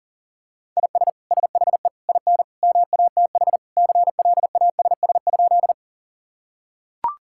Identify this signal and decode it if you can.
Morse code, heard as a keyed tone, about 40 wpm, 700 Hz. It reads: IS SHE IN MATH CLASS?
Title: Sentences from Top 1000 Words - Review Part 6 - 40wpm